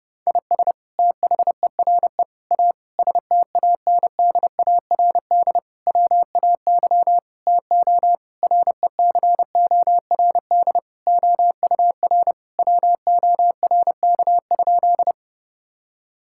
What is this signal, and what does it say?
Morse code, heard as a keyed tone, about 30 wpm, 700 Hz. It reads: IS THERE A STANDARD WAY TO RECORD OUR WORK?
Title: Sentences from Top 1000 Words - Review Part 6 - 30wpm